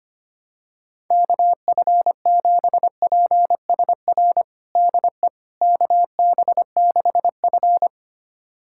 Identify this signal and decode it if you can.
Morse code, heard as a keyed tone, about 25 wpm, 700 Hz. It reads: KF7PSR DE KB6F
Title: USA Callsign DE Callsign 25wpm